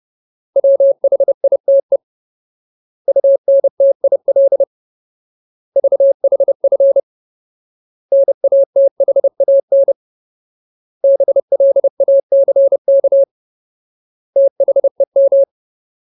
Morse code, heard as a keyed tone, about 30 wpm, 550 Hz. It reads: WHITE UNTIL VHF NATHAN BLACK THEM
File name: Ultimate Word List / 30WPM Ultimate